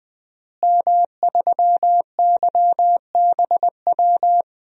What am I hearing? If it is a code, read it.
Morse code, heard as a keyed tone, about 20 wpm, 700 Hz. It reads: M3YBW
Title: DXCC-100 Call Signs - 5 Characters - Repeated 3x - 20wpm